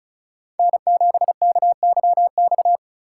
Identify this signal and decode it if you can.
Morse code, heard as a keyed tone, about 36 wpm, 700 Hz. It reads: N7KYX
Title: Speed Racing - USA Call Signs 30wpm